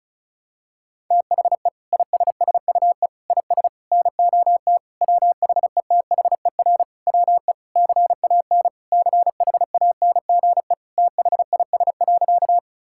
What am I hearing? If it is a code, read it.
Morse code, heard as a keyed tone, about 36 wpm, 700 Hz. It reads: THE ISSUE IS NOT WHETHER WE CAN CHANGE THIS.
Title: Sentences from Top 1000 Words - Review Part 6 - 35wpm